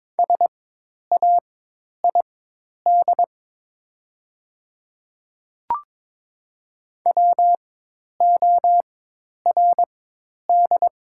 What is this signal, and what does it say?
Morse code, heard as a keyed tone, about 22 wpm, spaced as for 10 wpm, 690 Hz. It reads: SAID WORD
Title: 4x Character Spacing - Top 200 Words - 22wpm